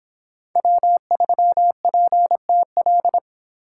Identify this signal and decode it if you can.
Morse code, heard as a keyed tone, about 26 wpm, 700 Hz. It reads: W3PTL